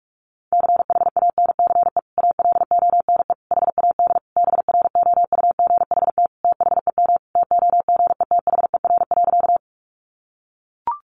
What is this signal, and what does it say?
Morse code, heard as a keyed tone, about 44 wpm, 700 Hz. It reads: CHANCE ALONE HAD BROUGHT THEM TOGETHER.
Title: Sentences from Top 1000 Words 45wpm